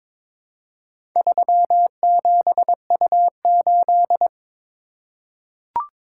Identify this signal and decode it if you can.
Morse code, heard as a keyed tone, about 22 wpm, 700 Hz. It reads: 37U8
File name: practice / 22wpm / Four Letter-Number Combinations - 22wpm